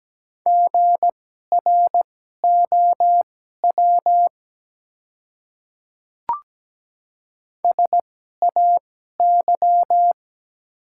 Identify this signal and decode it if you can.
Morse code, heard as a keyed tone, about 17 wpm, spaced as for 12 wpm, 700 Hz. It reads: GROW SAY
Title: 2x Character Spacing - Top 300 Words - 17wpm